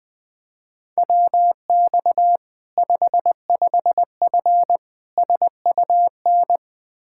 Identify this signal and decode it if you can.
Morse code, heard as a keyed tone, about 20 wpm, 700 Hz. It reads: WX 55F SUN
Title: K1USN SST QSO Practice 20wpm